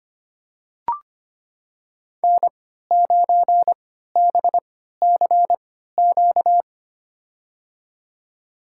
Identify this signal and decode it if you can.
Morse code, heard as a keyed tone, about 25 wpm, spaced as for 14 wpm, 700 Hz. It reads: N9BCQ
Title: USA Call Signs - 5 Characters - 3x Character Spacing - 25wpm